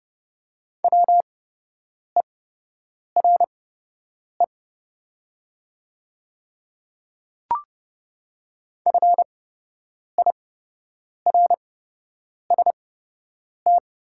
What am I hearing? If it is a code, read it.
Morse code, heard as a keyed tone, about 30 wpm, spaced as for 8 wpm, 700 Hz. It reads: WERE FIRST
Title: 8x Character Spacing - Top 100 Words - 30wpm